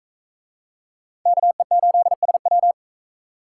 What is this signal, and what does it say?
Morse code, heard as a keyed tone, about 40 wpm, 700 Hz. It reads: KE8SW